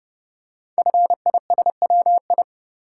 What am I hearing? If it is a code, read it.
Morse code, heard as a keyed tone, about 30 wpm, 700 Hz. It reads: FISWI
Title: Code Groups - 5 Letters - 30wpm